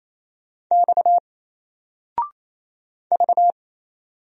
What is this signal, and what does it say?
Morse code, heard as a keyed tone, about 27 wpm, 700 Hz. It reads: X V
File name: Single-Letter-Number-28wpm